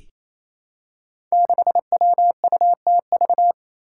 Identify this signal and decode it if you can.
Morse code, heard as a keyed tone, about 28 wpm, 700 Hz. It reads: 6WUTV
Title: Five Challenging Letter-Number Combinations 28wpm